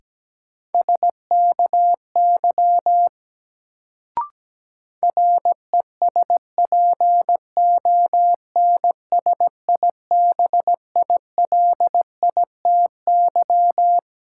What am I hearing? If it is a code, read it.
Morse code, heard as a keyed tone, about 17 wpm, 700 Hz. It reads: SKY RESPONSIBILITY
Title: Sentences from Top 1200 Words - New Word Drill Alt - 17wpm